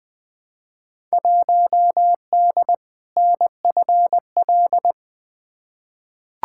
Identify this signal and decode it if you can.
Morse code, heard as a keyed tone, about 20 wpm, 700 Hz. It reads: 1D NFL
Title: Field Day Exchanges - 20wpm